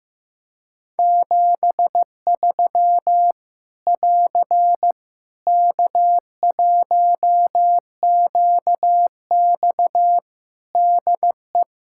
Morse code, heard as a keyed tone, about 16 wpm, 700 Hz. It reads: 73 <AR> K1QX DE
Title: K1USN SST QSO Practice 15wpm